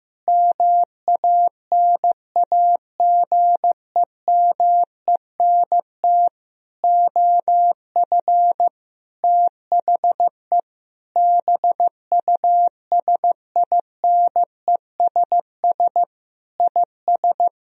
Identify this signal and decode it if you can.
Morse code, heard as a keyed tone, about 15 wpm, 700 Hz. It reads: MANAGEMENT OF THE BUSINESS IS
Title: Sentences from Top 1000 Words - Review Part 4 - 15wpm